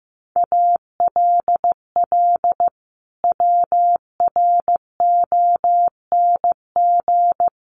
Morse code, heard as a keyed tone, about 15 wpm, 700 Hz. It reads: ALL WRONG
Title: Sentences from Top 900 Words 15wpm